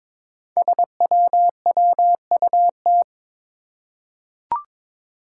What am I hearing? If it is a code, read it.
Morse code, heard as a keyed tone, about 22 wpm, 700 Hz. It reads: SWWUT
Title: Five Challenging Letter-Number Combinations 22wpm